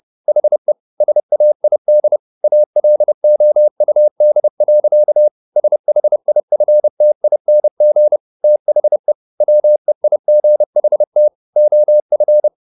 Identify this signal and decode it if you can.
Morse code, heard as a keyed tone, about 30 wpm, 600 Hz. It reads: HE SAID ALOUD. SHIFTING THE WEIGHT OF
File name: the old man and the sea / the old man and the sea 1 30wpm